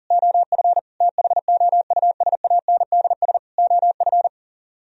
Morse code, heard as a keyed tone, about 52 wpm, 700 Hz. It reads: OF THOUSANDS OF